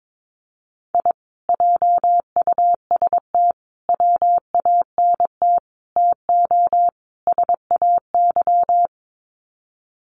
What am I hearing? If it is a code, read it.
Morse code, heard as a keyed tone, about 22 wpm, 700 Hz. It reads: I JUST WANT TO SAY
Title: Sets of 5 Words Encore 22wpm